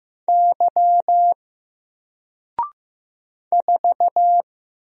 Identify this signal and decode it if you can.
Morse code, heard as a keyed tone, about 15 wpm, 700 Hz. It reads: Y 4
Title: Single Letter-Number 15wpm